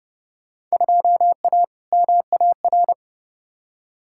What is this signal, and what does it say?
Morse code, heard as a keyed tone, about 30 wpm, 700 Hz. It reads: 2A MAR
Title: Field Day Exchanges - 30wpm